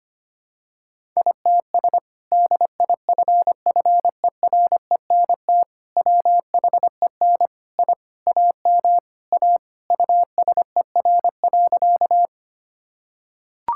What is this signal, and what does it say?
Morse code, heard as a keyed tone, about 25 wpm, 700 Hz. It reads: ITS DIFFERENT WHEN I AM A USER.
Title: Sentences from Top 1000 Words - Review Part 5 - 25wpm